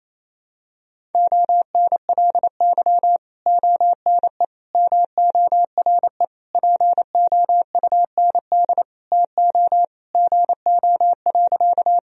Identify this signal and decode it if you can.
Morse code, heard as a keyed tone, about 28 wpm, 700 Hz. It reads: ONLY ONE MORE POUND TO GO.
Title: Sentences from Top 1000 Words - Review Part 2 - 28wpm